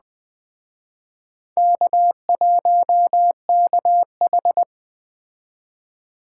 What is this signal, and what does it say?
Morse code, heard as a keyed tone, about 20 wpm, 700 Hz. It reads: K1KH
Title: Speed Racing - USA 1x2 Call Signs 17wpm